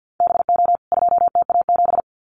Morse code, heard as a keyed tone, about 50 wpm, 700 Hz. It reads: BO 2TA7
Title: CWT Exchange 50wpm